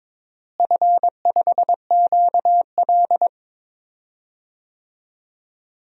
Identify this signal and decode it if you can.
Morse code, heard as a keyed tone, about 22 wpm, 700 Hz. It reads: F5QL
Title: DXCC-100 Call Signs - 4 Characters - Repeated 3x - 22wpm